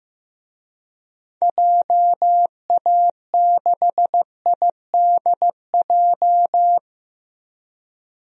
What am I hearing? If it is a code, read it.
Morse code, heard as a keyed tone, about 15 wpm, 700 Hz. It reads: JA6IDJ
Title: DXCC-100 Call Signs - 6 Characters - 15wpm